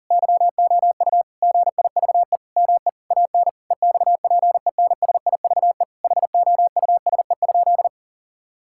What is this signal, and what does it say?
Morse code, heard as a keyed tone, about 37 wpm, 700 Hz. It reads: YOU GIVE ME AN EXPENSIVE HOUSE?
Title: Sentences from Top 1000 Words - Review Part 6 - 40wpm